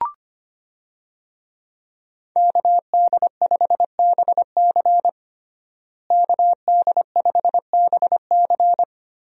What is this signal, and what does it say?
Morse code, heard as a keyed tone, about 25 wpm, 700 Hz. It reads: KD5BC KD5BC
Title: USA Call Signs - 5 Characters - Repeated 2x - 25wpm